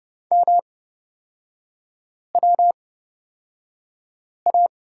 Morse code, heard as a keyed tone, about 30 wpm, spaced as for 5 wpm, 700 Hz. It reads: MWA
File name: Lesson 38e - Letter q - Three Characters 30wpm